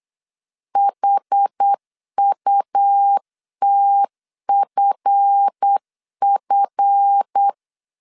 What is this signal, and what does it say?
Morse code, heard as a keyed tone, about 8 wpm, 800 Hz. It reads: HUTFF
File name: LEZIONI / 7 GF9Z / 7 GF 8wpm 3108